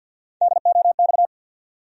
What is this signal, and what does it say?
Morse code, heard as a keyed tone, about 47 wpm, 700 Hz. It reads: DOX
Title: USA Call Sign Postfixes - 3 Letters 50wpm